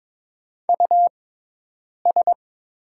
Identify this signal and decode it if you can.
Morse code, heard as a keyed tone, about 22 wpm, spaced as for 8 wpm, 700 Hz. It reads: US
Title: 6x Character Spacing - Top 200 Words - 22wpm